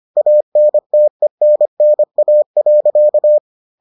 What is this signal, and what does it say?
Morse code, heard as a keyed tone, about 25 wpm, 600 Hz. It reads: ANTENNA.